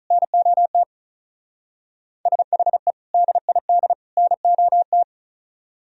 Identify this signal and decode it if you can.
Morse code, heard as a keyed tone, about 44 wpm, 700 Hz. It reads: NOT SHE DID NOT